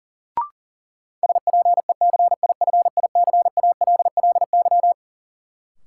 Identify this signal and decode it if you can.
Morse code, heard as a keyed tone, about 40 wpm, 700 Hz. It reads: SPECIFICALLY
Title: Sentences from Top 1100 Words - New Word Drill - 40wpm